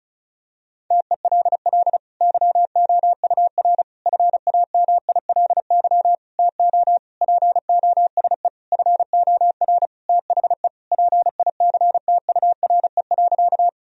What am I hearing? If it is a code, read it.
Morse code, heard as a keyed tone, about 34 wpm, 700 Hz. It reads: TELL YOUR FAMILY TO POSE FOR THE PICTURE.
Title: Sentences from Top 1000 Words - Review Part 2 - 35wpm